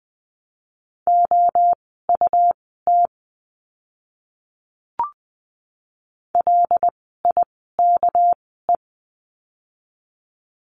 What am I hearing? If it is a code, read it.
Morse code, heard as a keyed tone, about 20 wpm, spaced as for 14 wpm, 700 Hz. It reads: OUT LIKE
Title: 206 Lesson 33f - Letter j - Words 20wpm